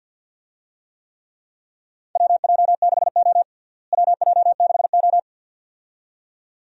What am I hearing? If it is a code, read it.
Morse code, heard as a keyed tone, about 50 wpm, 700 Hz. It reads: WJ6O WJ6O
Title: USA Call Signs - 4 Characters - Repeated 2x - 50wpm